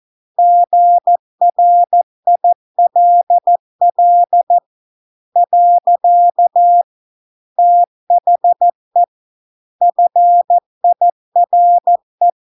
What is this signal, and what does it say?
Morse code, heard as a keyed tone, about 14 wpm, 700 Hz. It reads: GRILL . THE FIRE